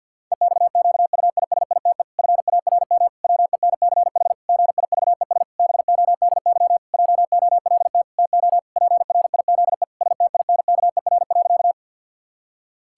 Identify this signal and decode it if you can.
Morse code, heard as a keyed tone, about 51 wpm, 700 Hz. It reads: EXQUISITE FARM WENCH GIVES BODY JOLT TO PRIZE STINKER.
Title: Pangram Challenge v2 - 50wpm